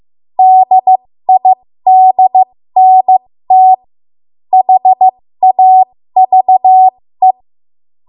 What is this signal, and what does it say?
Morse code, heard as a keyed tone, about 15 wpm, spaced as for 13 wpm, 750 Hz. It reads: DIDNT HAVE
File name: Archive / 13 WPM / 13WPM